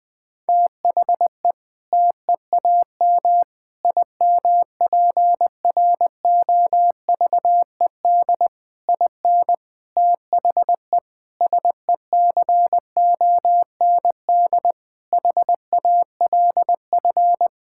Morse code, heard as a keyed tone, about 20 wpm, 700 Hz. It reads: THE TEAM IMPROVED IN THE SECOND HALF
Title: Sentences with 7-Word Phrases - 20wpm